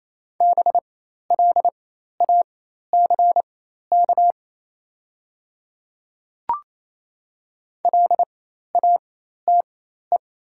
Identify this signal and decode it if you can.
Morse code, heard as a keyed tone, about 28 wpm, spaced as for 13 wpm, 690 Hz. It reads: BLACK LATE
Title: 4x Character Spacing - Top 500 Words - 28wpm